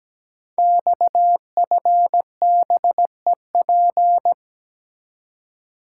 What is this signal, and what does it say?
Morse code, heard as a keyed tone, about 17 wpm, 700 Hz. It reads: XFBEP
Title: Code Groups - 5 Letter:Numbers - 17wpm